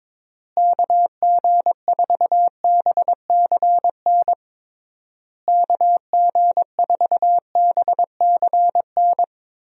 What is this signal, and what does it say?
Morse code, heard as a keyed tone, about 22 wpm, 700 Hz. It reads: KG4BCN KG4BCN
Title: USA Call Signs - 6 Characters - Repeated 3x - 22wpm